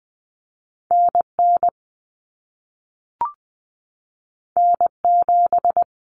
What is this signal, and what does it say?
Morse code, heard as a keyed tone, about 20 wpm, 700 Hz. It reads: NN N7
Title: Common Cut Numbers - 2-Digit 20wpm